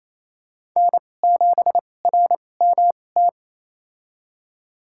Morse code, heard as a keyed tone, about 28 wpm, spaced as for 20 wpm, 700 Hz. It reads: N7RMT
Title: USA Call Signs - 2x Character Spacing - 28wpm